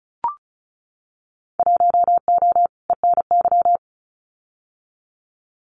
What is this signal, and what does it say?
Morse code, heard as a keyed tone, about 36 wpm, 700 Hz. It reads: 1O ENY
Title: Winter Field Day Exchanges 35wpm